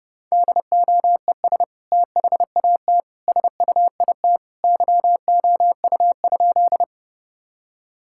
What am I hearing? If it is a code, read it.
Morse code, heard as a keyed tone, about 30 wpm, 700 Hz. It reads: DOES THAT SUIT YOU?
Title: Sentences from Top 1600 Words 30wpm